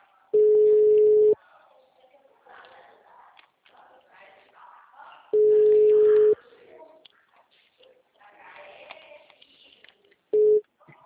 Ringback tone (Europe, 425 Hz).